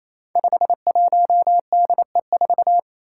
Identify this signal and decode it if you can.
Morse code, heard as a keyed tone, about 28 wpm, 700 Hz. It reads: 51DE4
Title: Five Challenging Letter-Number Combinations 28wpm